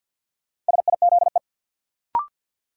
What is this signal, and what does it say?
Morse code, heard as a keyed tone, about 53 wpm, 700 Hz. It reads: SIZE
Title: Sentences from Top 400 Words - New Word Drill - 50wpm